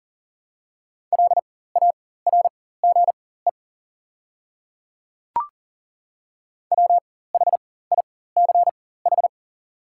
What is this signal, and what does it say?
Morse code, heard as a keyed tone, about 40 wpm, spaced as for 19 wpm, 690 Hz. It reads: LARGE WHICH
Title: 4x Character Spacing - Top 200 Words - 40wpm